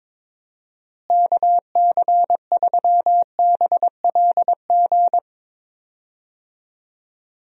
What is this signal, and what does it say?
Morse code, heard as a keyed tone, about 22 wpm, 700 Hz. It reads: KC3BLG
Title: USA Call Signs - 6 Characters - Repeated 3x - 22wpm